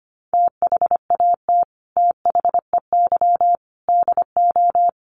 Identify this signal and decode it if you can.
Morse code, heard as a keyed tone, about 25 wpm, 700 Hz. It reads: THAT THEY DO
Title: Sets of 5 Words Encore2 25wpm